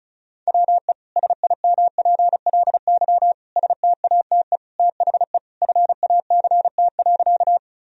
Morse code, heard as a keyed tone, about 34 wpm, 700 Hz. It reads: WE SIMPLY STATE THE FACT.
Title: Sentences from Top 1000 Words - Review Part 4 - 35wpm